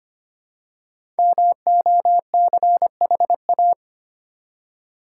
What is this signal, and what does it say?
Morse code, heard as a keyed tone, about 25 wpm, 690 Hz. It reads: MOCHA
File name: practice / 25wpm / Easy Words with Hints - 25wpm